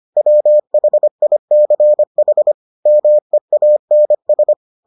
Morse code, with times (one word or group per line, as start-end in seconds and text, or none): 0.2-2.5 WHICH
2.9-4.5 MEANS